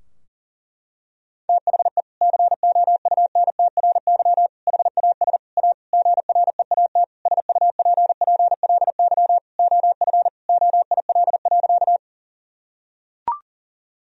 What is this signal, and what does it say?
Morse code, heard as a keyed tone, about 40 wpm, 700 Hz. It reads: THE COUNTRY HAS A GREAT SUPPLY OF OIL.
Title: Sentences from Top 1000 Words - Review Part 4 - 40wpm